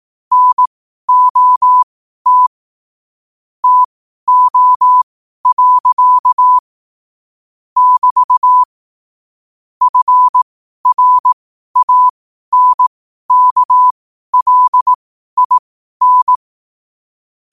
Morse code, heard as a keyed tone, about 18 wpm, spaced as for 12 wpm, 1000 Hz. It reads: NOT TO. <BT> FRANKLIN
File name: qotdfiles / 12wpm